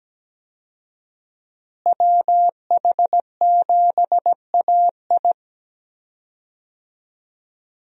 Morse code, heard as a keyed tone, about 17 wpm, 700 Hz. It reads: WH7AI
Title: USA Call Signs - 5 Characters - Repeated 2x - 17wpm